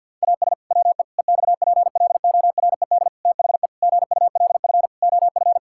Morse code, heard as a keyed tone, about 50 wpm, 700 Hz. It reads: AS WE EXPLORED THE GULF OF
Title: Pangram Challenge v2 - 50wpm